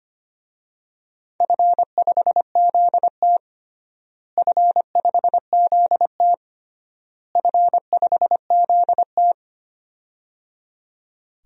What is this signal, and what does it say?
Morse code, heard as a keyed tone, about 25 wpm, 700 Hz. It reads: F5ZT F5ZT F5ZT
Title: DXCC-100 Call Signs - 4 Characters - Repeated 3x - 25wpm